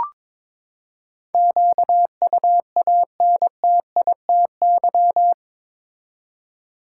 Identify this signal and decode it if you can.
Morse code, heard as a keyed tone, about 22 wpm, 700 Hz. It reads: QUANTITY
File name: practice / 22wpm / Eight Letter Words - 22wpm